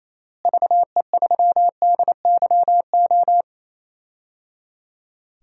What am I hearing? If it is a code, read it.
Morse code, heard as a keyed tone, about 27 wpm, 700 Hz. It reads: VE3DYO
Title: DXCC-100 Call Signs - 6 Characters - Repeated 3x - 28wpm